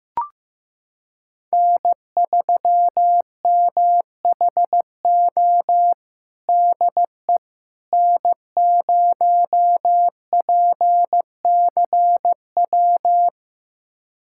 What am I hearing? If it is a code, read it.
Morse code, heard as a keyed tone, about 15 wpm, 700 Hz. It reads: N3MHO DE N0PCW
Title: USA Callsign DE Callsign - 5 Characters - 15wpm